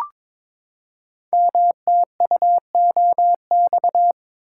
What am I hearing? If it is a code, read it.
Morse code, heard as a keyed tone, about 22 wpm, 700 Hz. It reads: MTUOX
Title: Code Groups - 5 Letters - 22wpm